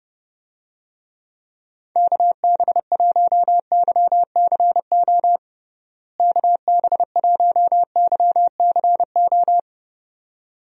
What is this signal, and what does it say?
Morse code, heard as a keyed tone, about 32 wpm, 700 Hz. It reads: KB1YCO KB1YCO
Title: USA Call Signs - 6 Characters - Repeated 2x - 30wpm